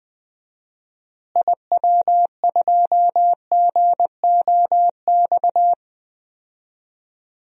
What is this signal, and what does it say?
Morse code, heard as a keyed tone, about 20 wpm, 700 Hz. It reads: IW2GOX
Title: DXCC-100 Call Signs - 6 Characters - Repeated 2x - 20wpm